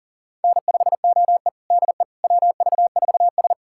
Morse code, heard as a keyed tone, about 37 wpm, 700 Hz. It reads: N5OE DE WV4S
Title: USA Callsign DE Callsign - 4 Characters - 40wpm